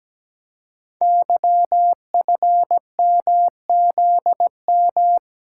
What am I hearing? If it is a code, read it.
Morse code, heard as a keyed tone, about 17 wpm, 700 Hz. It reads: YFMZM
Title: Code Groups - 5 Letters - 17wpm